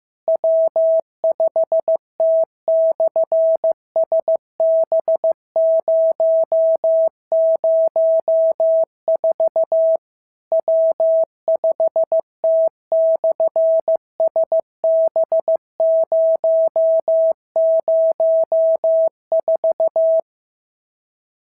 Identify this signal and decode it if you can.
Morse code, heard as a keyed tone, about 15 wpm, 650 Hz. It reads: W5T/SB004 W5T/SB004
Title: SOTA QSO Practice 15wpm